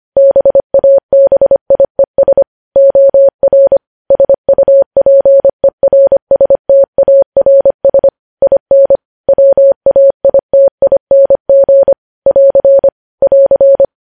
Morse code, heard as a keyed tone, about 25 wpm, 550 Hz. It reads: BABIES OR SUPERSTARS IN WAITING <AR> <AR>
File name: podcast / News headlines-1x-25x25